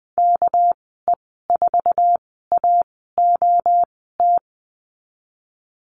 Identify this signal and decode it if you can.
Morse code, heard as a keyed tone, about 20 wpm, spaced as for 15 wpm, 700 Hz. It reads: KE4AOT
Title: 200 Lesson 32g - Letter k - Call Signs 20wpm